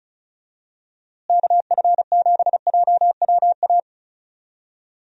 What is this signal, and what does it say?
Morse code, heard as a keyed tone, about 36 wpm, 700 Hz. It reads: KF7JWA
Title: USA Call Signs - 6 Characters - Repeated 2x - 35wpm